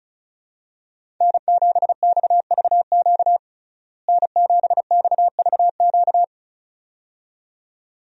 Morse code, heard as a keyed tone, about 34 wpm, 700 Hz. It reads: N7XVQ N7XVQ
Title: USA Call Signs - 5 Characters - Repeated 2x - 35wpm